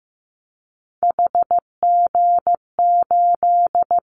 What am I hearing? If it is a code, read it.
Morse code, heard as a keyed tone, about 15 wpm, 700 Hz. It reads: HG8
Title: Callsign Prefixes for Top 50 DXCC - 15wpm